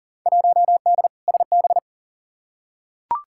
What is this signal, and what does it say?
Morse code, heard as a keyed tone, about 40 wpm, 700 Hz. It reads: 1D SB
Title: Field Day Exchanges - 40wpm